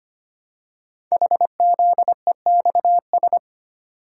Morse code, heard as a keyed tone, about 25 wpm, 700 Hz. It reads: HZEXS
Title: Code Groups - 5 Letters - 25wpm